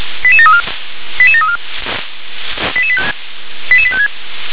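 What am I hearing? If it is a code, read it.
Five-tone selective call (ZVEI), heard as quick runs of five tones, tones of about 80 ms. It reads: 88734 88734 88856
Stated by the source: ZVEI five-tone selective call received on an SDR.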